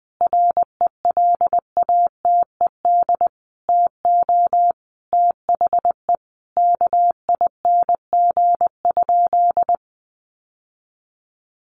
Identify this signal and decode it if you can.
Morse code, heard as a keyed tone, about 20 wpm, 700 Hz. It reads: RELATED TO THE KING?
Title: Sentences from Top 1400 Words 20wpm